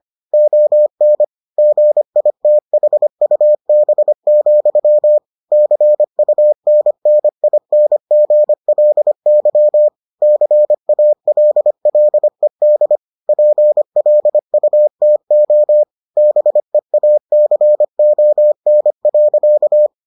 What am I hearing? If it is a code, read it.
Morse code, heard as a keyed tone, about 25 wpm, 600 Hz. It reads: ON GITHUB, CUNNINGLY CALLED PLUTO BEACON.